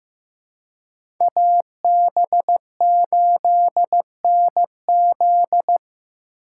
Morse code, heard as a keyed tone, about 15 wpm, 700 Hz. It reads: AB8NZ
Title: USA Call Signs - 5 Characters - 4x Character Spacing - 15wpm